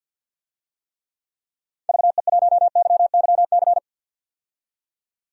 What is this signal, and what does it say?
Morse code, heard as a keyed tone, about 51 wpm, 700 Hz. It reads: VE1YYC
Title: DXCC-100 Call Signs - 6 Characters - 50wpm